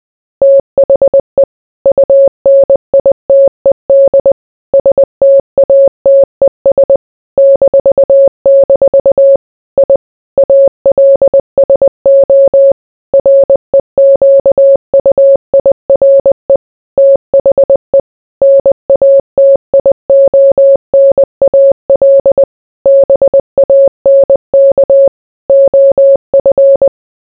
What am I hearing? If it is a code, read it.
Morse code, heard as a keyed tone, about 20 wpm, 550 Hz. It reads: THE UNITED STATES -- I ALSO REQUIRE THE NATIONAL BANK OF